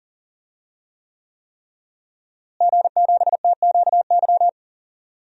Morse code, heard as a keyed tone, about 37 wpm, 700 Hz. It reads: G7TQY
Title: DXCC-100 Call Signs - 5 Characters - Repeated 3x - 40wpm